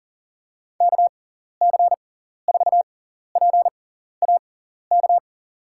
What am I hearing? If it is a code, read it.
Morse code, heard as a keyed tone, about 40 wpm, spaced as for 14 wpm, 700 Hz. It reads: KC4PAK
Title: USA Call Signs - 6 Characters - 6x Character Spacing - 40wpm